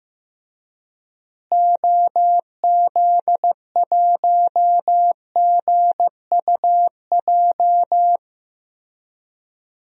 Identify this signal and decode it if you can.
Morse code, heard as a keyed tone, about 16 wpm, 700 Hz. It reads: OZ1GUJ
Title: DXCC-100 Call Signs - 6 Characters - Repeated 3x - 15wpm